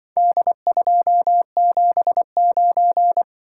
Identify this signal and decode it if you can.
Morse code, heard as a keyed tone, about 24 wpm, 700 Hz. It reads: D279
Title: Speed-Racing - Four Letter-Number Combinations - 20wpm